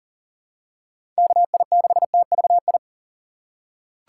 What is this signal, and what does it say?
Morse code, heard as a keyed tone, about 40 wpm, 700 Hz. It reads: KI6TVI
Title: USA Call Signs - 6 Characters - 6x Character Spacing - 40wpm